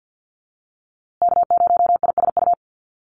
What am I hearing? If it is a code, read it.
Morse code, heard as a keyed tone, about 50 wpm, 700 Hz. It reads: K0ISU
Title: USA Call Signs Repeated 2x 50wpm